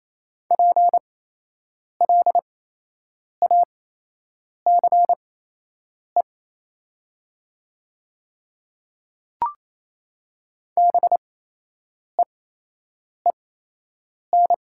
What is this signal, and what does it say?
Morse code, heard as a keyed tone, about 28 wpm, spaced as for 8 wpm, 700 Hz. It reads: PLACE BEEN
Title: 8x Character Spacing - Top 200 Words - 28wpm